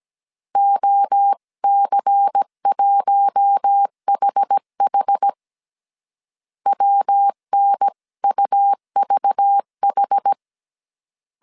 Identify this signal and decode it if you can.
Morse code, heard as a keyed tone, about 17 wpm, 800 Hz. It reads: OC1HH WNUVH
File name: CW / DET / 1 6 1500C S A 16wpm 2635